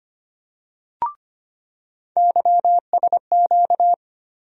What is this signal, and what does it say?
Morse code, heard as a keyed tone, about 25 wpm, 700 Hz. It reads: YSQ
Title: USA Call Sign Postfixes - 3 Letters 25wpm